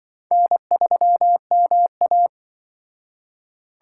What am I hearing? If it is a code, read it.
Morse code, heard as a keyed tone, about 24 wpm, 700 Hz. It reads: N3MA
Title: Speed Racing - USA 1x2 Call Signs 20wpm